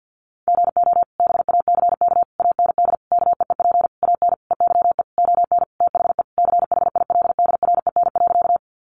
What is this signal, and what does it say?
Morse code, heard as a keyed tone, about 50 wpm, 700 Hz. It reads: GO BACK AND KEEP AN EYE ON THE CHILDREN.
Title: Sentences from Top 300 Words 50wpm